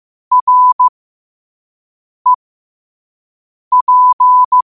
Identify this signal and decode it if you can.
Morse code, heard as a keyed tone, about 15 wpm, spaced as for 5 wpm, 1000 Hz. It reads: REP
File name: qotdfiles / 5wpm